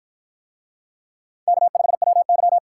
Morse code, heard as a keyed tone, about 49 wpm, 700 Hz. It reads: K5WY